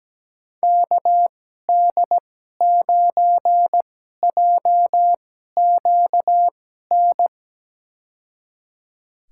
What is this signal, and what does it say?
Morse code, heard as a keyed tone, about 17 wpm, spaced as for 12 wpm, 700 Hz. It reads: KD9JQN
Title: USA Call Signs - 6 Characters - 2x Character Spacing - 17wpm